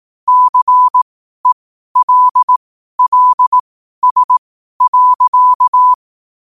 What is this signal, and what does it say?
Morse code, heard as a keyed tone, about 18 wpm, spaced as for 12 wpm, 1000 Hz. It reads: CELLS.
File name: qotdfiles / 12wpm